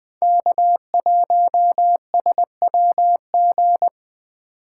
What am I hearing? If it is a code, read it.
Morse code, heard as a keyed tone, about 20 wpm, 700 Hz. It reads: K1SWG
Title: USA Call Signs - 5 Characters - 4x Character Spacing - 20wpm